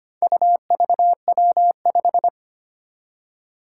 Morse code, heard as a keyed tone, about 25 wpm, 700 Hz. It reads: UVW5
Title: Four Challenging Letter-Number Combinations 25wpm